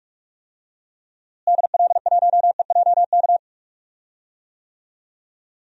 Morse code, heard as a keyed tone, about 43 wpm, 700 Hz. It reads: DL1EJK
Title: DXCC-100 Call Signs - 6 Characters - Repeated 3x - 45wpm